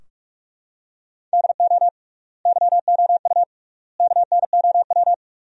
Morse code, heard as a keyed tone, about 46 wpm, 690 Hz. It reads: DO YOU KNOW
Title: 3x Word-Spacing - Sentences from Top 100 Words - 45wpm